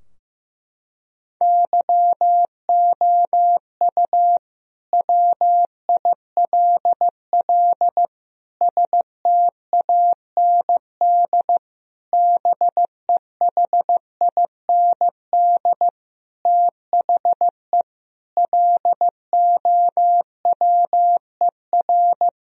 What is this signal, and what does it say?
Morse code, heard as a keyed tone, about 15 wpm, 700 Hz. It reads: YOU WILL STAND BEHIND THE LOWER
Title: Sentences from Top 1000 Words - Review Part 4 - 15wpm